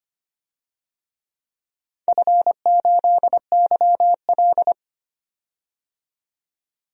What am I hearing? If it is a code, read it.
Morse code, heard as a keyed tone, about 25 wpm, 700 Hz. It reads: F8YL
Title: DXCC-100 Call Signs - 4 Characters - 25wpm